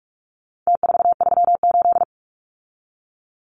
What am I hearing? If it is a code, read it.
Morse code, heard as a keyed tone, about 44 wpm, 700 Hz. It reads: T438
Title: Numbers - Contest Style - 4-Digit 45wpm